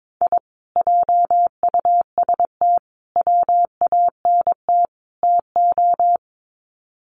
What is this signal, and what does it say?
Morse code, heard as a keyed tone, about 22 wpm, 700 Hz. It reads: I JUST WANT TO
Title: Sets of 4 Words 22wpm